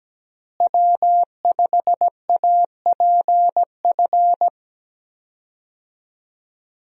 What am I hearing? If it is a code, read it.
Morse code, heard as a keyed tone, about 17 wpm, 700 Hz. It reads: W5APF
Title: USA Call Signs - 5 Characters - Repeated 3x - 17wpm